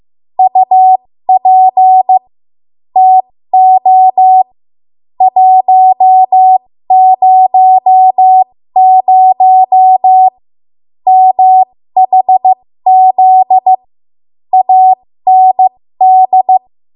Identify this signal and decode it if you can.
Morse code, heard as a keyed tone, about 15 wpm, spaced as for 13 wpm, 750 Hz. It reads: UP TO 100 MHZ AND
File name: Morse / 13WPM